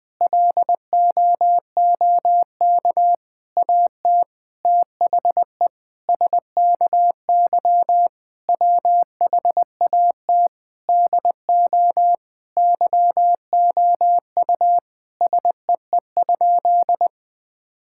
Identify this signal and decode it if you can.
Morse code, heard as a keyed tone, about 20 wpm, 700 Hz. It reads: LOOK AT THE SKY WHAT DO YOU SEE?
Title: Sentences with 5-Word Phrases - 20wpm